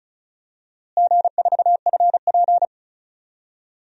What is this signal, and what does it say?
Morse code, heard as a keyed tone, about 34 wpm, 700 Hz. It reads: G4FP
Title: DXCC-100 Call Signs - 4 Characters - Repeated 2x - 35wpm